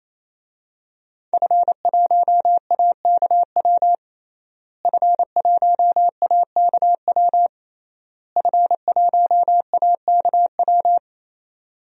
Morse code, heard as a keyed tone, about 28 wpm, 700 Hz. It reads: F1AKW F1AKW F1AKW
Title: DXCC-100 Call Signs - 5 Characters - Repeated 3x - 28wpm